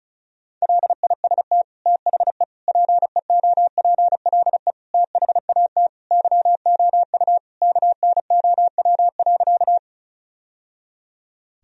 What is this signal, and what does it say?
Morse code, heard as a keyed tone, about 34 wpm, 700 Hz. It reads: LIST THE PEOPLE THAT YOU KNOW.
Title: Sentences from Top 1000 Words - Review Part 2 - 35wpm